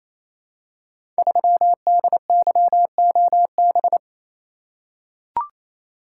Morse code, heard as a keyed tone, about 27 wpm, 700 Hz. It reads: 3DYOB
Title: Code Groups - 5 Letter:Numbers - 28wpm